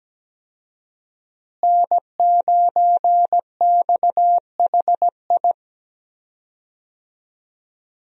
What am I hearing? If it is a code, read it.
Morse code, heard as a keyed tone, about 17 wpm, 700 Hz. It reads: N9XHI
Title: USA Call Signs - 5 Characters - Repeated 3x - 17wpm